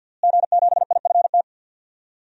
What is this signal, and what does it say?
Morse code, heard as a keyed tone, about 47 wpm, 700 Hz. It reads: G7IFT
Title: DXCC-100 Call Signs - 5 Characters - Repeated 3x - 50wpm